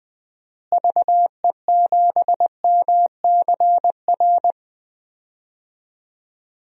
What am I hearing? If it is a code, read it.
Morse code, heard as a keyed tone, about 20 wpm, 700 Hz. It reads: VE7MCR
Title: DXCC-100 Call Signs - 6 Characters - Repeated 3x - 20wpm